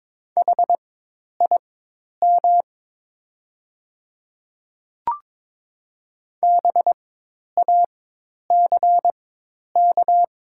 Morse code, heard as a keyed tone, about 22 wpm, spaced as for 10 wpm, 700 Hz. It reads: HIM BACK
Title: 4x Character Spacing - Top 300 Words - 22wpm